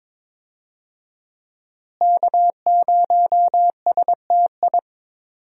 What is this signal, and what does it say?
Morse code, heard as a keyed tone, about 22 wpm, 700 Hz. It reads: K0STI
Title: USA Call Signs - 5 Characters - Repeated 2x - 22wpm